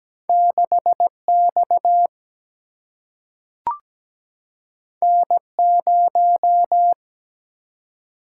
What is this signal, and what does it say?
Morse code, heard as a keyed tone, about 17 wpm, 700 Hz. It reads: 6X N0
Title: Single-Letter Single-Number 17wpm